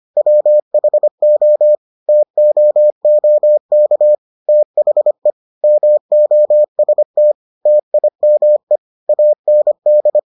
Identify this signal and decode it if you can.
Morse code, heard as a keyed tone, about 25 wpm, 600 Hz. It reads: WHO TOOK THE MOST TIME AND